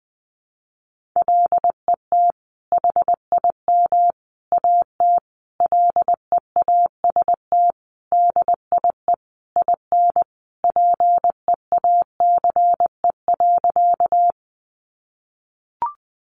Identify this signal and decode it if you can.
Morse code, heard as a keyed tone, about 20 wpm, 700 Hz. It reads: LET HIM AT LEAST DIE IN PEACE.
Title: Sentences from Top 1500 Words 20wpm